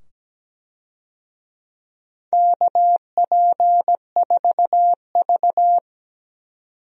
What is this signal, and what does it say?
Morse code, heard as a keyed tone, about 17 wpm, 700 Hz. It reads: KP4V
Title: DXCC-100 Call Signs - 4 Characters - 17wpm